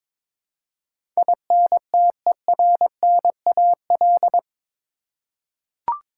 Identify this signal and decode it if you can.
Morse code, heard as a keyed tone, about 22 wpm, 700 Hz. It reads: INTERNAL
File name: practice / 22wpm / Eight Letter Words - 22wpm